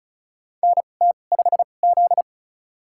Morse code, heard as a keyed tone, about 36 wpm, spaced as for 25 wpm, 700 Hz. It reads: NT5Z
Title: USA Call Signs - 4 Characters - 2x Character Spacing - 35wpm